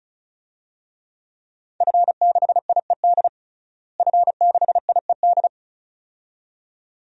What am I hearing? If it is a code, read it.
Morse code, heard as a keyed tone, about 34 wpm, 700 Hz. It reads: F6IED F6IED
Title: DXCC-100 Call Signs - 5 Characters - Repeated 2x - 35wpm